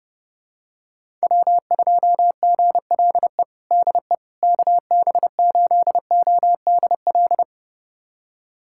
Morse code, heard as a keyed tone, about 30 wpm, 700 Hz. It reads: W2GLE DE KB8ODL
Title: USA Callsign DE Callsign 30wpm